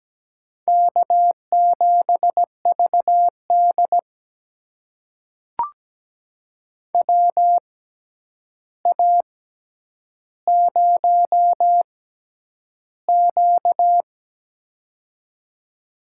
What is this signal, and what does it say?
Morse code, heard as a keyed tone, about 17 wpm, spaced as for 9 wpm, 700 Hz. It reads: K7VD WA0Q
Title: USA Call Signs - 4 Characters - 6x Character Spacing - 17wpm